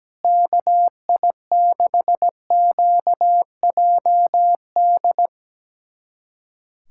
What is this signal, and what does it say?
Morse code, heard as a keyed tone, about 17 wpm, 700 Hz. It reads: KI6QJD
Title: USA Callsign DE Callsign - 6 Characters - 17wpm